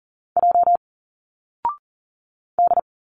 Morse code, heard as a keyed tone, about 40 wpm, 700 Hz. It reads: J D
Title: Single Letters 40wpm